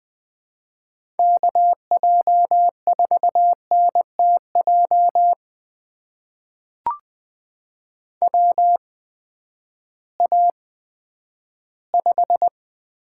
Morse code, heard as a keyed tone, about 20 wpm, spaced as for 11 wpm, 700 Hz. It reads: KJ4NTJ WA5
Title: USA Call Signs - 6 Characters - 8x Character Spacing - 20wpm